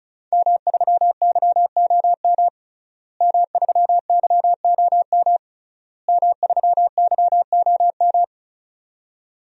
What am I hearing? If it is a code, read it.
Morse code, heard as a keyed tone, about 36 wpm, 700 Hz. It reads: M3YOM M3YOM M3YOM
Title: DXCC-100 Call Signs - 5 Characters - Repeated 3x - 35wpm